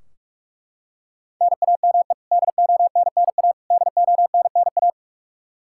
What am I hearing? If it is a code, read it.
Morse code, heard as a keyed tone, about 41 wpm, 700 Hz. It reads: NAME DONNA DONNA
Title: Practice QSO - Exchange 1 - Beginning - 45wpm